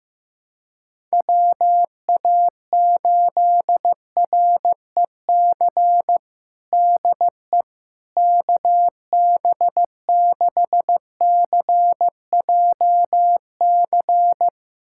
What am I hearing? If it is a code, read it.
Morse code, heard as a keyed tone, about 15 wpm, 700 Hz. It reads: WA8REC DE KB6CJC
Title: USA Callsign DE Callsign - 6 Characters - 15wpm